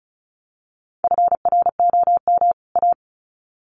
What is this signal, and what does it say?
Morse code, heard as a keyed tone, about 34 wpm, 700 Hz. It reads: FROM A
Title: Sets of 2 Words 35wpm